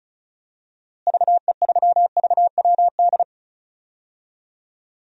Morse code, heard as a keyed tone, about 36 wpm, 700 Hz. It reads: VE3VWD
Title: DXCC-100 Call Signs - 6 Characters - Repeated 3x - 35wpm